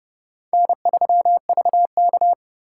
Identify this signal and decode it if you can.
Morse code, heard as a keyed tone, about 28 wpm, 700 Hz. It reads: N3VK